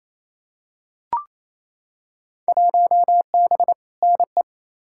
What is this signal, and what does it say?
Morse code, heard as a keyed tone, about 28 wpm, 700 Hz. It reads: 1B NE
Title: Field Day Exchanges Encore - 28wpm